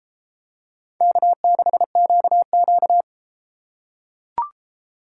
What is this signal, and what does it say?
Morse code, heard as a keyed tone, about 32 wpm, 700 Hz. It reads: K6QQ